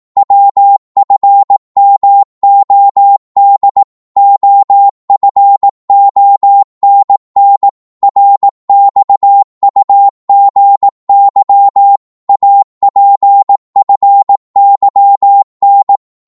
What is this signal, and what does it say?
Morse code, heard as a keyed tone, about 18 wpm, 800 Hz. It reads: WFMOD OFONN RXUGY APFYN
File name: Morse / ltr-18wpm